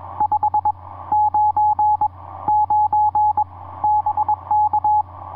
Morse code, heard as a keyed tone, about 21 wpm, 840 Hz. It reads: 5 9 9 BK
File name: dx1j dl6kai etc 40m cw 13jan2010 2235